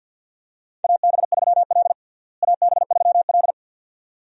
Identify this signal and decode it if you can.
Morse code, heard as a keyed tone, about 53 wpm, 700 Hz. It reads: AB3L AB3L
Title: USA Call Signs - 4 Characters - Repeated 2x - 50wpm